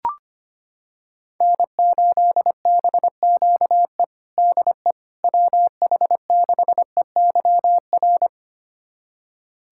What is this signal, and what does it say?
Morse code, heard as a keyed tone, about 25 wpm, 700 Hz. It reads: N8BQE DE WH6EYR
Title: USA Callsign DE Callsign 25wpm